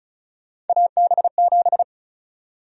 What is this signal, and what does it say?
Morse code, heard as a keyed tone, about 36 wpm, 700 Hz. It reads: AB7
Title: USA Call Sign Prefixes - Repeated 2x - 35wpm